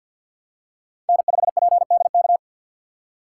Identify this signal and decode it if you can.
Morse code, heard as a keyed tone, about 52 wpm, 700 Hz. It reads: N5PDK
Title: USA Call Signs - 5 Characters - 2x Character Spacing - 50wpm